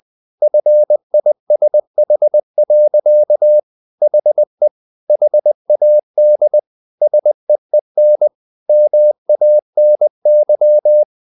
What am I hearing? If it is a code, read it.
Morse code, heard as a keyed tone, about 20 wpm, 600 Hz. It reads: FISH. HE HAD SEEN MANY
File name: the old man and the sea / the old man and the sea part 1 20wpm